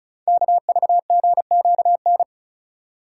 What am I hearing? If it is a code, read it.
Morse code, heard as a keyed tone, about 35 wpm, 690 Hz. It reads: KVGQN
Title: Code Groups - 5 Letters - 35wpm